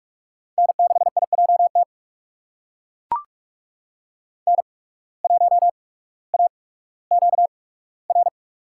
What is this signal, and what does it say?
Morse code, heard as a keyed tone, about 41 wpm, spaced as for 19 wpm, 700 Hz. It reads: N6IJT N1AQR
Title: USA Call Signs - 5 Characters - 8x Character Spacing - 45wpm